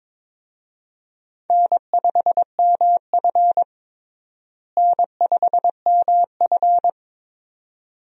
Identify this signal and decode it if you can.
Morse code, heard as a keyed tone, about 22 wpm, 700 Hz. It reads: N5MF N5MF
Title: USA Call Signs - 4 Characters - Repeated 2x - 22wpm